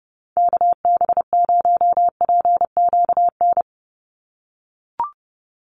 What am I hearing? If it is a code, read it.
Morse code, heard as a keyed tone, about 30 wpm, 700 Hz. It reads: KB0PQN
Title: USA Call Signs 30wpm